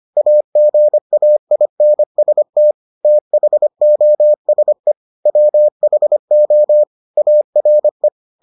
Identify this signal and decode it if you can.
Morse code, heard as a keyed tone, about 25 wpm, 600 Hz. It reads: AGAINST THOSE WHO ARE